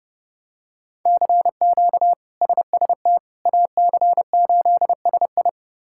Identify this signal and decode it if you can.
Morse code, heard as a keyed tone, about 30 wpm, 700 Hz. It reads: CQ SST AC8SI
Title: K1USN SST QSO Practice - 30wpm